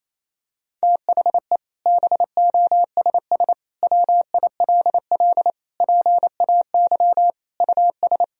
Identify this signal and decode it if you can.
Morse code, heard as a keyed tone, about 28 wpm, 700 Hz. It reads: THE BOSS WILL PAY US
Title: Sentences from Top 1000 Words - Review Part 4 - 28wpm